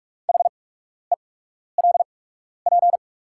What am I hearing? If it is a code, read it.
Morse code, heard as a keyed tone, about 43 wpm, spaced as for 12 wpm, 690 Hz. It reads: HELP
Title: 8x Character Spacing - Top 300 Words - 45wpm